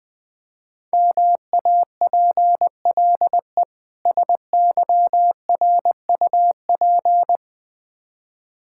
Morse code, heard as a keyed tone, about 21 wpm, 700 Hz. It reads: MAPLE SYRUP
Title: Difficult Words with Hints - 20wpm